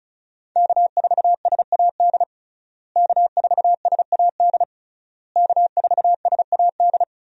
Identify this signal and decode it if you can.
Morse code, heard as a keyed tone, about 36 wpm, 700 Hz. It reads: K4SAD K4SAD K4SAD
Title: USA Call Signs - 5 Characters - Repeated 3x - 35wpm